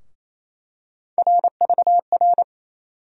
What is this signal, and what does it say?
Morse code, heard as a keyed tone, about 28 wpm, 700 Hz. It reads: RVR
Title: USA Call Sign Postfixes - 3 Letters 28wpm